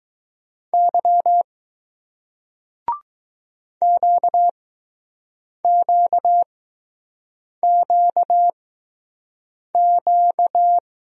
Morse code, heard as a keyed tone, about 20 wpm, spaced as for 6 wpm, 700 Hz. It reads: Y QQQQ